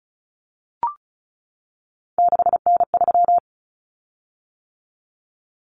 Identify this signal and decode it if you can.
Morse code, heard as a keyed tone, about 34 wpm, 700 Hz. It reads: 6N3
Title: Common Cut Numbers - 3-Digit 35wpm